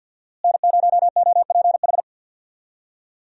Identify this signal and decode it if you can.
Morse code, heard as a keyed tone, about 46 wpm, 700 Hz. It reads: N0OPH
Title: USA Call Signs - 5 Characters - Repeated 3x - 50wpm